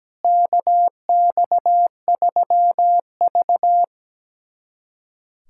Morse code, heard as a keyed tone, about 17 wpm, 700 Hz. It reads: KX3V
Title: USA Callsign DE Callsign - 4 Characters - 17wpm